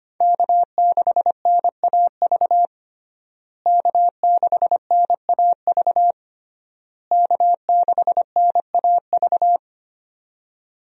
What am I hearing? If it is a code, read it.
Morse code, heard as a keyed tone, about 25 wpm, 700 Hz. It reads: K6NAV K6NAV K6NAV
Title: USA Call Signs - 5 Characters - Repeated 3x - 25wpm